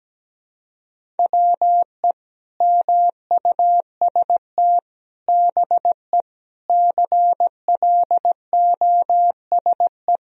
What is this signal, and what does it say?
Morse code, heard as a keyed tone, about 17 wpm, 700 Hz. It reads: WE MUST BE CLOSE
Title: Sentences from Top 1000 Words - Review Part 1 - 17wpm